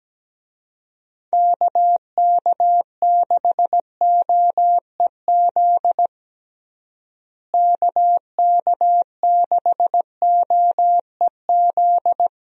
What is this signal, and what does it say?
Morse code, heard as a keyed tone, about 17 wpm, 700 Hz. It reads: KK6OEZ KK6OEZ
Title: USA Call Signs - 6 Characters - Repeated 3x - 17wpm